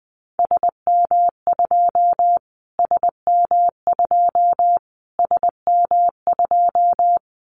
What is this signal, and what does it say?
Morse code, heard as a keyed tone, about 20 wpm, 700 Hz. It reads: SM2 SM2 SM2
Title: Callsign Prefixes for Top 25 DXCC - Repeated 3x - 20wpm